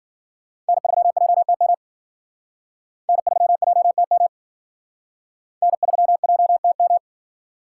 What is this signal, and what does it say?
Morse code, heard as a keyed tone, about 52 wpm, 700 Hz. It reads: N3JTM N3JTM N3JTM